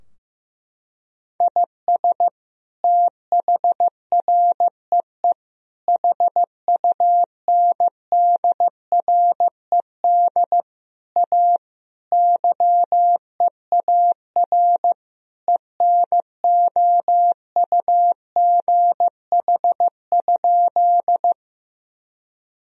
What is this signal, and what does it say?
Morse code, heard as a keyed tone, about 15 wpm, 700 Hz. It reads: IS THREE HUNDRED A YEAR ENOUGH?
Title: Sentences from Top 1000 Words - Review Part 2 - 15wpm